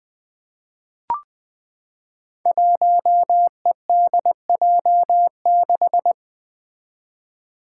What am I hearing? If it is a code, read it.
Morse code, heard as a keyed tone, about 21 wpm, 700 Hz. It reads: 1EDJ6
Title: Five Challenging Letter-Number Combinations 20wpm